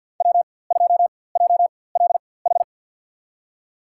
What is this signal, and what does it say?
Morse code, heard as a keyed tone, about 49 wpm, spaced as for 23 wpm, 700 Hz. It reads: W2JLH
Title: USA Call Signs - 5 Characters - 4x Character Spacing - 50wpm